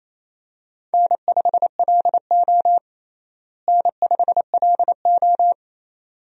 Morse code, heard as a keyed tone, about 28 wpm, 700 Hz. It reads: N5LO N5LO
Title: USA Call Signs - 4 Characters - Repeated 2x - 28wpm